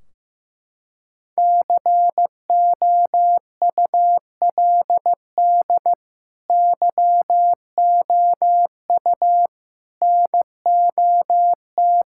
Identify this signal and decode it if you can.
Morse code, heard as a keyed tone, about 16 wpm, 700 Hz. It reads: COULD YOU NOT
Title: Sentences from Top 1600 Words 15wpm